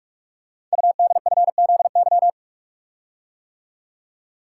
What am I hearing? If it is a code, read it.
Morse code, heard as a keyed tone, about 44 wpm, 700 Hz. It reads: UDFZY
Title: Code Groups - 5 Letters - 45wpm